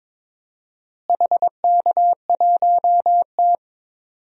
Morse code, heard as a keyed tone, about 22 wpm, 700 Hz. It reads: HK1T
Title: DXCC-100 Call Signs - 4 Characters - Repeated 2x - 22wpm